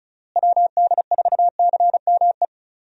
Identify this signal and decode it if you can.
Morse code, heard as a keyed tone, about 34 wpm, 700 Hz. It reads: WD4CME
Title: USA Call Signs - 6 Characters - Repeated 3x - 35wpm